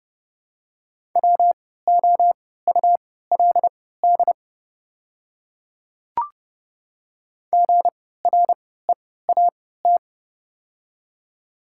Morse code, heard as a keyed tone, about 32 wpm, spaced as for 17 wpm, 700 Hz. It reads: WOULD GREAT
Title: 3x Character Spacing - Top 300 Words - 30wpm